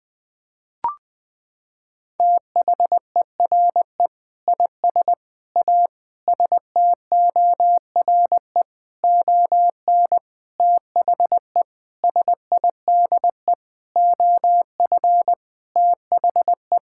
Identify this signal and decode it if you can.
Morse code, heard as a keyed tone, about 21 wpm, 700 Hz. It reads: THERE IS A STORE ON THE SIDE OF THE
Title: Sentences with 6-Word Phrases - 20wpm